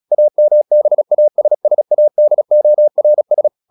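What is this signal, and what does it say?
Morse code, heard as a keyed tone, about 36 wpm, 600 Hz. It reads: AMBASSADORS
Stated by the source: Morse practice at 36 wpm, text of Ken Follett's novel Never, from CW med Gnister.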